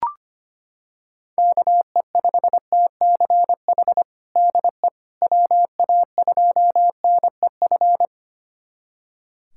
Morse code, heard as a keyed tone, about 25 wpm, 700 Hz. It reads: KE5TCH DE WA2NEF
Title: USA Callsign DE Callsign 25wpm